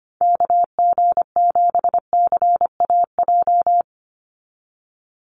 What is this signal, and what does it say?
Morse code, heard as a keyed tone, about 25 wpm, 700 Hz. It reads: KG7CAJ
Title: USA Call Signs 25wpm